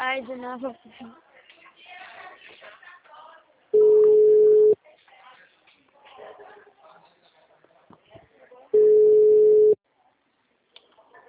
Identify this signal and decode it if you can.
Ringback tone (Europe, 425 Hz), with speech.